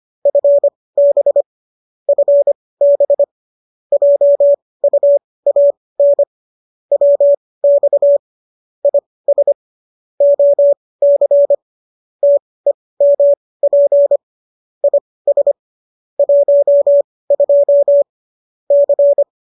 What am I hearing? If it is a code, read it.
Morse code, heard as a keyed tone, about 25 wpm, spaced as for 18 wpm, 570 Hz. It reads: FB FB JUAN WX IS OC TEMP IS 12 C
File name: qso204 18